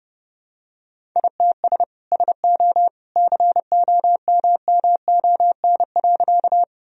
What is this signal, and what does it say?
Morse code, heard as a keyed tone, about 30 wpm, 700 Hz. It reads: ITS SO COMMON.
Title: Sentences from Top 1000 Words - Review Part 3 - 30wpm